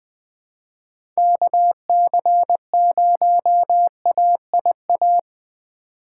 Morse code, heard as a keyed tone, about 20 wpm, 700 Hz. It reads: KC0AIA
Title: USA Call Signs - 6 Characters - 4x Character Spacing - 20wpm